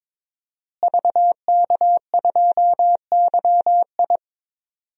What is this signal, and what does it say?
Morse code, heard as a keyed tone, about 22 wpm, 700 Hz. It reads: VK2YI
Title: DXCC-100 Call Signs - 5 Characters - Repeated 2x - 22wpm